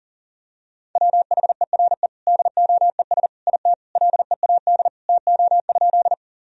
Morse code, heard as a keyed tone, about 40 wpm, 700 Hz. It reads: WHERE DOES IT LEAD TO?
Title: Sentences from Top 1000 Words - Review Part 2 - 40wpm